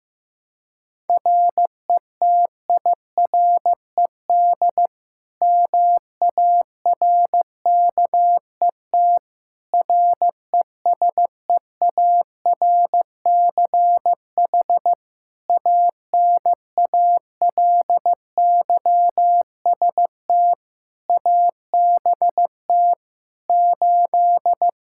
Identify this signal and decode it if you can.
Morse code, heard as a keyed tone, about 16 wpm, 700 Hz. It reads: RETIRED MARKET RESEARCH ANALYST ABT 8
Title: Practice QSO - Exchange 3 - More Info - 15wpm